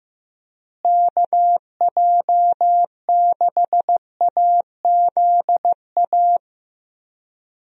Morse code, heard as a keyed tone, about 15 wpm, 700 Hz. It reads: KJ6AZA
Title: USA Call Signs - 6 Characters - Repeated 3x - 15wpm